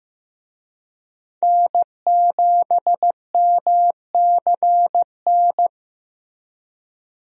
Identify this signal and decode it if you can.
Morse code, heard as a keyed tone, about 15 wpm, 700 Hz. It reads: N7MCN
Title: USA Call Signs - 5 Characters - Repeated 2x - 15wpm